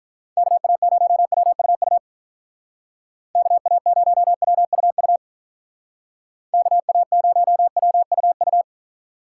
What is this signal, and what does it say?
Morse code, heard as a keyed tone, about 45 wpm, 700 Hz. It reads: KA0WUU KA0WUU KA0WUU